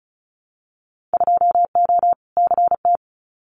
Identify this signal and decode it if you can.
Morse code, heard as a keyed tone, about 36 wpm, 700 Hz. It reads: 2O CT
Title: Winter Field Day Exchanges 35wpm